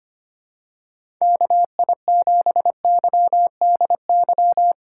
Morse code, heard as a keyed tone, about 25 wpm, 700 Hz. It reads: KI7YDY
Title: USA Call Signs - 2x Character Spacing - 25wpm